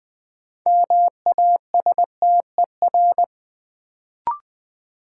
Morse code, heard as a keyed tone, about 20 wpm, 700 Hz. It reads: MASTER
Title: Sentences from Top 1300 Words - New Word Drill - 20wpm